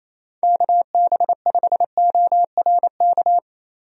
Morse code, heard as a keyed tone, about 27 wpm, 700 Hz. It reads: KB5ORK
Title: USA Call Signs - 6 Characters - Repeated 2x - 28wpm